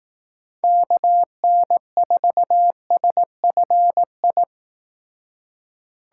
Morse code, heard as a keyed tone, about 18 wpm, 700 Hz. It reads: KN4SFI
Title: Speed Racing - USA 2x3 Call Signs 15wpm